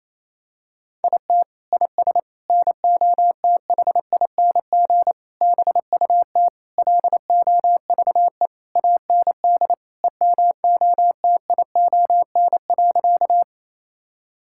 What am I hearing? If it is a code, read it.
Morse code, heard as a keyed tone, about 28 wpm, 700 Hz. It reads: IT IS NOTHING BUT LOVE AND EMOTION.
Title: Sentences from Top 1600 Words 28wpm